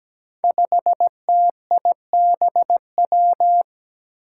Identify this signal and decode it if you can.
Morse code, heard as a keyed tone, about 17 wpm, 700 Hz. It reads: 5TIBW
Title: Five Challenging Letter-Number Combinations 17wpm